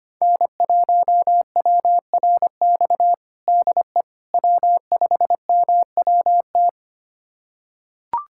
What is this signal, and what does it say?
Morse code, heard as a keyed tone, about 25 wpm, 700 Hz. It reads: N1WRX DE W5MWT
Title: USA Callsign DE Callsign - 5 Characters - 25wpm